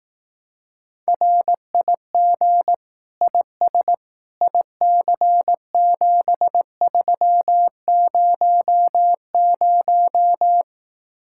Morse code, heard as a keyed tone, about 18 wpm, 700 Hz. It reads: RIG IS IC7300
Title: Speed-Racing Common QSO Phrases Encore - 15wpm